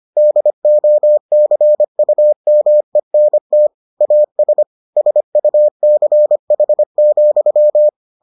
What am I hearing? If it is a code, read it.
Morse code, heard as a keyed tone, about 25 wpm, 600 Hz. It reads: DOCUMENT AS SUCH,